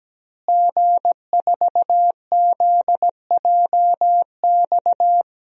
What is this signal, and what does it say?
Morse code, heard as a keyed tone, about 17 wpm, 700 Hz. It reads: G4ZJX
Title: DXCC-100 Call Signs - 5 Characters - Repeated 3x - 17wpm